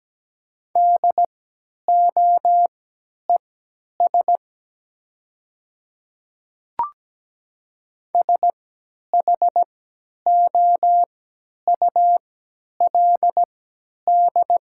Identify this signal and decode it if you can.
Morse code, heard as a keyed tone, about 17 wpm, spaced as for 10 wpm, 700 Hz. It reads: DOES SHOULD
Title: 3x Character Spacing - Top 300 Words - 17wpm